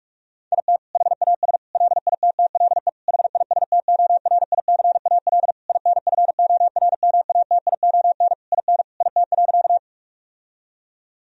Morse code, heard as a keyed tone, about 41 wpm, 700 Hz. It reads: IT HAS LITTLE HISTORICAL INFORMATION IN IT.
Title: Sentences from Top 1000 Words - Review Part 4 - 45wpm